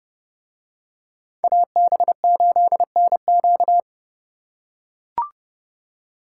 Morse code, heard as a keyed tone, about 29 wpm, 700 Hz. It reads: AB8NQ